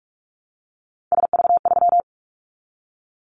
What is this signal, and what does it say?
Morse code, heard as a keyed tone, about 44 wpm, 700 Hz. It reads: SV3
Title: Callsign Prefixes for Top 50 DXCC - 45wpm